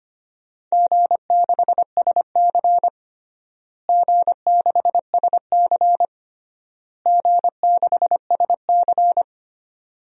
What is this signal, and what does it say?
Morse code, heard as a keyed tone, about 25 wpm, 700 Hz. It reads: G6SC G6SC G6SC
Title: DXCC-100 Call Signs - 4 Characters - Repeated 3x - 25wpm